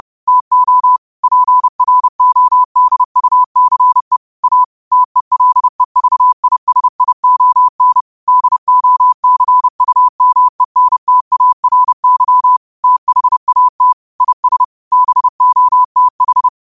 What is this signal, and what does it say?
Morse code, heard as a keyed tone, about 30 wpm, 1000 Hz. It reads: TO PRODUCE A TELEVISION DOCUMENTARY THAT IS BOTH
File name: 30wpm